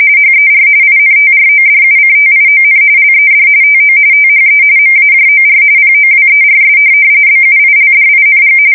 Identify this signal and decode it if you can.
RTTY, about 45.45 baud. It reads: HER INTRODUCTION LED TO THE MAJOR POWERS BUILDING
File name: fsk170 / USS-New-Jersey